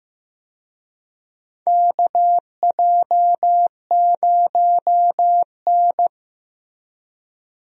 Morse code, heard as a keyed tone, about 15 wpm, 700 Hz. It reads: KJ0N
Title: USA Call Signs - 4 Characters - Repeated 3x - 15wpm